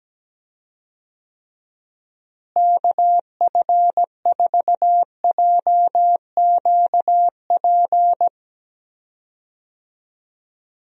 Morse code, heard as a keyed tone, about 17 wpm, 700 Hz. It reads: KF4JQP
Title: USA Call Signs - 6 Characters - Repeated 2x - 17wpm